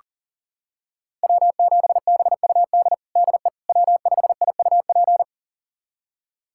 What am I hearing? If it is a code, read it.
Morse code, heard as a keyed tone, about 40 wpm, 700 Hz. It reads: W7BUD DE W5IUP
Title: USA Callsign DE Callsign - 5 Characters - 40wpm